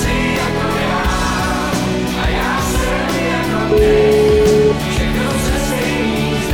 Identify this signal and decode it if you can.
Ringback tone (Europe, 425 Hz), with noise.